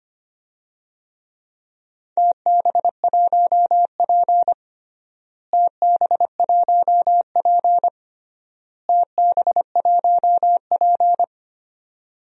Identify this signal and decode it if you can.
Morse code, heard as a keyed tone, about 25 wpm, 700 Hz. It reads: TB1P TB1P TB1P
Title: DXCC-100 Call Signs - 4 Characters - Repeated 3x - 25wpm